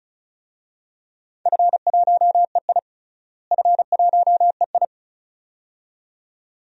Morse code, heard as a keyed tone, about 34 wpm, 700 Hz. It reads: F1EI F1EI
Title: DXCC-100 Call Signs - 4 Characters - Repeated 2x - 35wpm